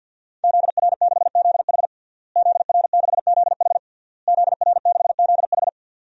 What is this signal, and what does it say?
Morse code, heard as a keyed tone, about 49 wpm, 700 Hz. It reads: ZR6ZH ZR6ZH ZR6ZH
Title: DXCC-100 Call Signs - 5 Characters - Repeated 3x - 50wpm